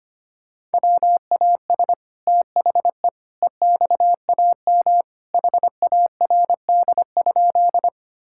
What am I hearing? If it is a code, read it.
Morse code, heard as a keyed tone, about 25 wpm, 700 Hz. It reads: WAS THE EXAM HARD?
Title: Sentences from Top 1000 Words - Review Part 5 - 25wpm